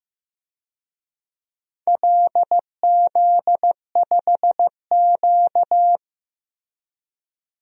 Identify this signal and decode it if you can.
Morse code, heard as a keyed tone, about 15 wpm, 700 Hz. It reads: LZ5Q
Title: DXCC-100 Call Signs - 4 Characters - Repeated 2x - 15wpm